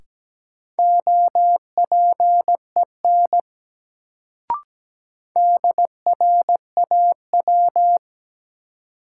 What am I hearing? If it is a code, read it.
Morse code, heard as a keyed tone, about 17 wpm, 700 Hz. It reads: OPEN DRAW
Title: Sentences from Top 300 Words - New Word Drill - 17wpm